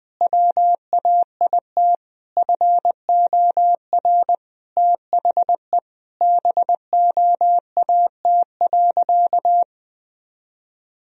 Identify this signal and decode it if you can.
Morse code, heard as a keyed tone, about 19 wpm, 700 Hz. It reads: WAIT FOR THE BOAT.
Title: Sentences from Top 1000 Words - Review Part 3 - 20wpm